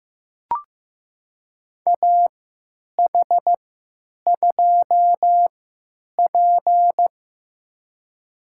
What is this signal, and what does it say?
Morse code, heard as a keyed tone, about 15 wpm, spaced as for 9 wpm, 700 Hz. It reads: AH2P
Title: USA Call Signs - 3x Character Spacing - 15wpm